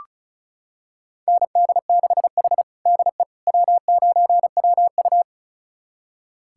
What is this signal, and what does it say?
Morse code, heard as a keyed tone, about 36 wpm, 700 Hz. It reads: ND6H DE W9WU
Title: USA Callsign DE Callsign - 4 Characters - 35wpm